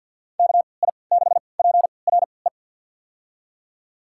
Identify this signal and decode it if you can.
Morse code, heard as a keyed tone, about 50 wpm, spaced as for 28 wpm, 700 Hz. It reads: KI6PRE
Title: USA Call Signs - 6 Characters - 3x Character Spacing - 50wpm